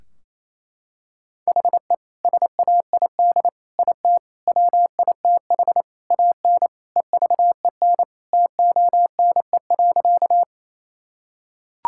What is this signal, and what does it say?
Morse code, heard as a keyed tone, about 28 wpm, 700 Hz. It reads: HE SAID IT WITH AN EVEN TONE.
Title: Sentences from Top 1600 Words 28wpm